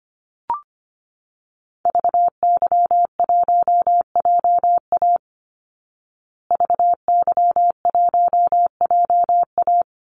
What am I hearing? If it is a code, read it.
Morse code, heard as a keyed tone, about 25 wpm, 700 Hz. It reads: VY1JA VY1JA
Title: Call Signs Repeated 3x - 25wpm